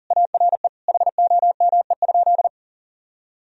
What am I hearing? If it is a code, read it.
Morse code, heard as a keyed tone, about 40 wpm, 700 Hz. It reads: ARE HOME?
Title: Sentences from Top 1000 Words - Review Part 1 - 40wpm